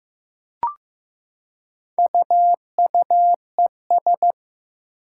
Morse code, heard as a keyed tone, about 15 wpm, 700 Hz. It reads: UUES
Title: Four Challenging Letter-Number Combinations 15wpm